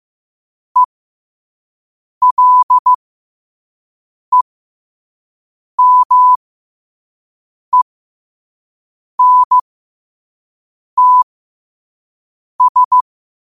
Morse code, heard as a keyed tone, about 15 wpm, spaced as for 5 wpm, 1000 Hz. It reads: ELEMENTS
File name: qotdfiles / 5wpm